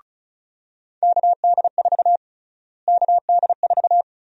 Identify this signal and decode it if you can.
Morse code, heard as a keyed tone, about 36 wpm, 700 Hz. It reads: KD4 KD4
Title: USA Call Sign Prefixes - Repeated 2x - 35wpm